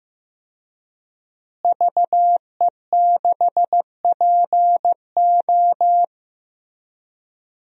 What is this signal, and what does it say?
Morse code, heard as a keyed tone, about 15 wpm, 700 Hz. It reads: VE6PO
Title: DXCC-100 Call Signs - 5 Characters - Repeated 3x - 15wpm